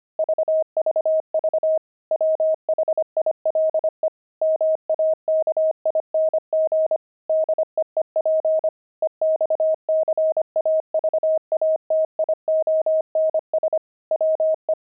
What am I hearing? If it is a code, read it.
Morse code, heard as a keyed tone, about 25 wpm, 630 Hz. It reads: VVV WHILE MAKING DEEP EXCAVATIONS WE
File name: pangram-fs-25-cs-25-deep